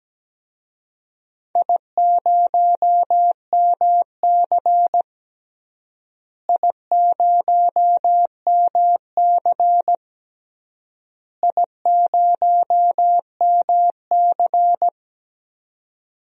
Morse code, heard as a keyed tone, about 17 wpm, 700 Hz. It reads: I0MC I0MC I0MC
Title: DXCC-100 Call Signs - 4 Characters - Repeated 3x - 17wpm